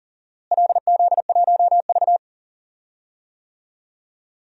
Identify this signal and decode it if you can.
Morse code, heard as a keyed tone, about 39 wpm, 700 Hz. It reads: LZ1V
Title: DXCC-100 Call Signs - 4 Characters - Repeated 2x - 40wpm